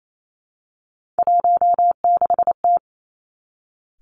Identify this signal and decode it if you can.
Morse code, heard as a keyed tone, about 28 wpm, 700 Hz. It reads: 16T
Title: Three Challenging Letter-Number Combinations 28wpm